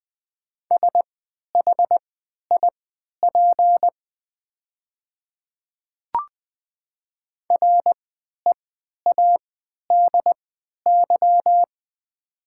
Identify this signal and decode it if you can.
Morse code, heard as a keyed tone, about 20 wpm, spaced as for 11 wpm, 700 Hz. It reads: SHIP READY
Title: 3x Character Spacing - Top 500 Words - 20wpm